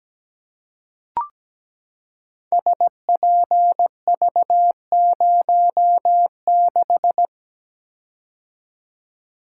Morse code, heard as a keyed tone, about 17 wpm, 700 Hz. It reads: SPV06
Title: Code Groups - 5 Letter:Numbers - 17wpm